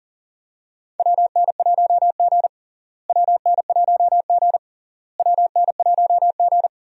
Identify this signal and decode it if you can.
Morse code, heard as a keyed tone, about 40 wpm, 700 Hz. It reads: WN1G WN1G WN1G
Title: USA Call Signs - 4 Characters - Repeated 3x - 40wpm